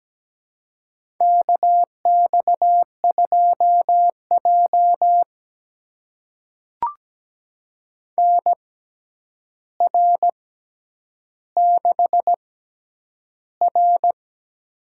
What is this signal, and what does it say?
Morse code, heard as a keyed tone, about 17 wpm, spaced as for 9 wpm, 700 Hz. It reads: KX2J NR6R
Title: USA Call Signs - 4 Characters - 6x Character Spacing - 17wpm